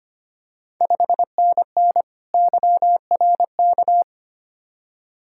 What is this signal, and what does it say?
Morse code, heard as a keyed tone, about 25 wpm, 700 Hz. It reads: 5NN YRK
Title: Ontario QSO Party Exchange 25wpm